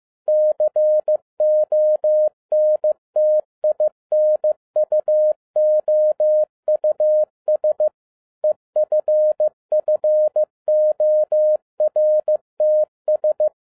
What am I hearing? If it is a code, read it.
Morse code, heard as a keyed tone, about 15 wpm, 610 Hz. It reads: CONTINUOUS EFFORTS
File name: 2012-01-mp3 / n17-SUMMARY OF FEA NET IN 2011-15wpm